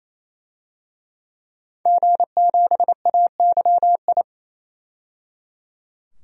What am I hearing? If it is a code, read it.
Morse code, heard as a keyed tone, about 28 wpm, 700 Hz. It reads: G7AYI
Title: DXCC-100 Call Signs - 5 Characters - 28wpm